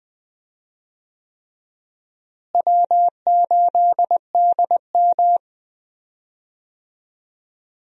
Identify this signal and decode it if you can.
Morse code, heard as a keyed tone, about 21 wpm, 700 Hz. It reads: W8DM
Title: USA Call Signs - 4 Characters - Repeated 3x - 20wpm